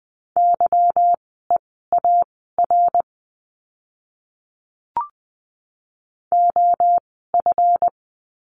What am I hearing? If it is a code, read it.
Morse code, heard as a keyed tone, about 20 wpm, spaced as for 15 wpm, 700 Hz. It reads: YEAR OF
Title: 244 Lesson 39f - Letter z - Words 20wpm